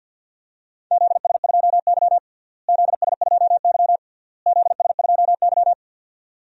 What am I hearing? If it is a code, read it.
Morse code, heard as a keyed tone, about 50 wpm, 700 Hz. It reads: ZS2Y ZS2Y ZS2Y
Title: DXCC-100 Call Signs - 4 Characters - Repeated 3x - 50wpm